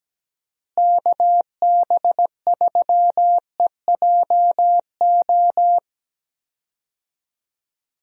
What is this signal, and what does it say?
Morse code, heard as a keyed tone, about 17 wpm, 700 Hz. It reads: KB3EJO
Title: USA Call Signs - 6 Characters - Repeated 3x - 17wpm